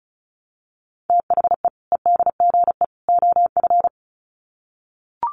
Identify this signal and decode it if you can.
Morse code, heard as a keyed tone, about 34 wpm, 700 Hz. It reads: THE EDGE OF
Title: Sets of 3 Words 35wpm